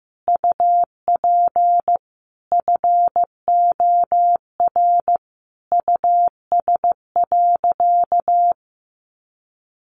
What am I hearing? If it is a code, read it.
Morse code, heard as a keyed tone, about 15 wpm, 700 Hz. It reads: UP FOR US.
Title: Sentences from Top 1400 Words 15wpm